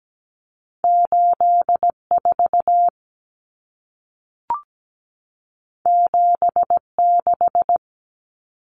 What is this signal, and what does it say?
Morse code, heard as a keyed tone, about 17 wpm, 700 Hz. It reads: 84 76
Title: Numbers - 2-Digit 17wpm